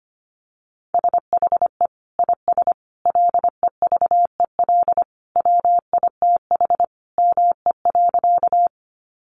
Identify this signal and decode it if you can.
Morse code, heard as a keyed tone, about 25 wpm, 700 Hz. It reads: SHE IS LEVEL WITH ME.
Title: Sentences from Top 700 Words 25wpm